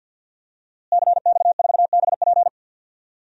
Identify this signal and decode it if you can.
Morse code, heard as a keyed tone, about 51 wpm, 700 Hz. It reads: CX4BP
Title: DXCC-100 Call Signs - 5 Characters - Repeated 3x - 50wpm